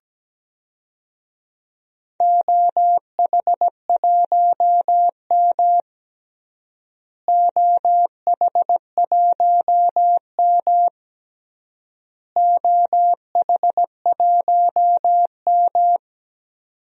Morse code, heard as a keyed tone, about 17 wpm, 700 Hz. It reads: OH1M OH1M OH1M
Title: DXCC-100 Call Signs - 4 Characters - Repeated 3x - 17wpm